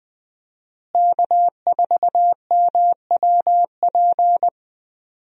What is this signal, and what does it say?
Morse code, heard as a keyed tone, about 20 wpm, 700 Hz. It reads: K4MWP
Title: USA Call Signs - 5 Characters - Repeated 3x - 20wpm